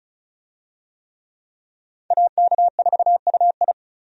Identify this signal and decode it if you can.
Morse code, heard as a keyed tone, about 34 wpm, 700 Hz. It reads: AK4UI
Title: USA Call Signs - 5 Characters - Repeated 3x - 35wpm